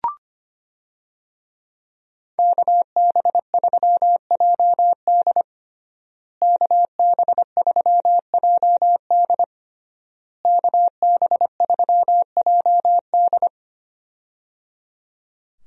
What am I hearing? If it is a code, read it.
Morse code, heard as a keyed tone, about 25 wpm, 700 Hz. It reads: KB3JD KB3JD KB3JD
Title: USA Call Signs - 5 Characters - Repeated 3x - 25wpm